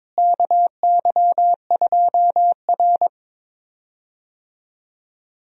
Morse code, heard as a keyed tone, about 21 wpm, 700 Hz. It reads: KY2R
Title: USA Call Signs - 4 Characters - Repeated 3x - 22wpm